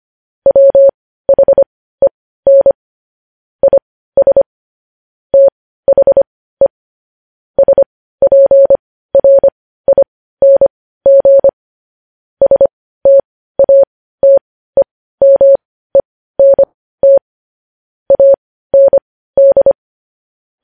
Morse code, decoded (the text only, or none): WHEN IS THE SPRING STATEMENT AND